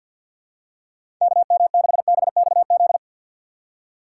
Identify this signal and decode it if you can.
Morse code, heard as a keyed tone, about 47 wpm, 700 Hz. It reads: KM6BXZ
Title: USA Call Signs - 6x Character Spacing - 50wpm